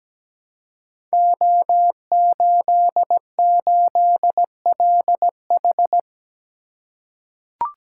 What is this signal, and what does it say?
Morse code, heard as a keyed tone, about 17 wpm, 700 Hz. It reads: O88LH
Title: Code Groups - 5 Letter:Numbers - 17wpm